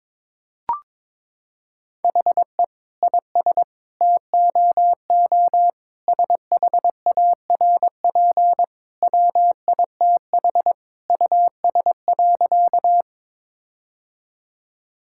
Morse code, heard as a keyed tone, about 21 wpm, 700 Hz. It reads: HE IS TOO SHARP WITH US.
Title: Sentences from Top 1600 Words 22wpm